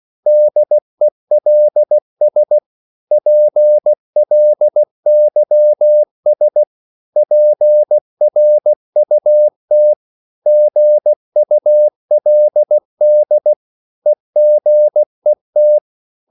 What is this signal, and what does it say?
Morse code, heard as a keyed tone, about 16 wpm, 600 Hz. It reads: DELS PLYS PRUT GULD EGET
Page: Korte danske ord 16wpm